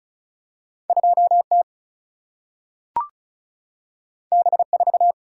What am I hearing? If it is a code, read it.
Morse code, heard as a keyed tone, about 36 wpm, 700 Hz. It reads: 2T B4
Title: Single-Letter Single-Number 35wpm